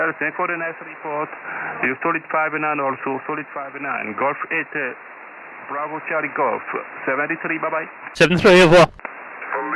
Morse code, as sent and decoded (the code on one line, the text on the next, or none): none